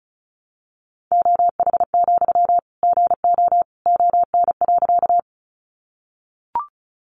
Morse code, heard as a keyed tone, about 34 wpm, 700 Hz. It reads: OH, GO ON.
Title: Sentences from Top 500 Words 35wpm